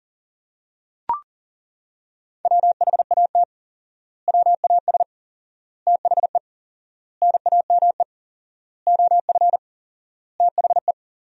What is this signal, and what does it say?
Morse code, heard as a keyed tone, about 40 wpm, 700 Hz. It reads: WHAT WAS THE NAME OF THE
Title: 4x Word-Spacing - Sentences from Top 300 Words - 40wpm